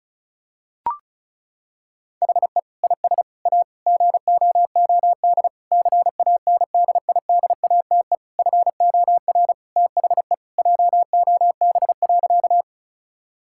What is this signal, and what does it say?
Morse code, heard as a keyed tone, about 36 wpm, 700 Hz. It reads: HE IS A GOOD CANDIDATE FOR THE JOB.
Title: Sentences from Top 1600 Words 35wpm